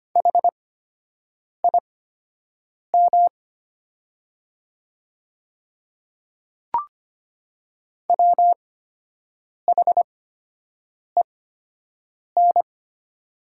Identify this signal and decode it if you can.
Morse code, heard as a keyed tone, about 25 wpm, spaced as for 7 wpm, 700 Hz. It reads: HIM WHEN
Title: 8x Character Spacing - Top 100 Words - 25wpm